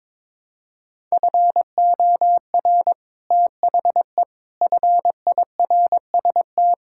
Morse code, heard as a keyed tone, about 22 wpm, 700 Hz. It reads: FOR THE FIRST
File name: practice / 22wpm / Sets of 6 Words - 22wpm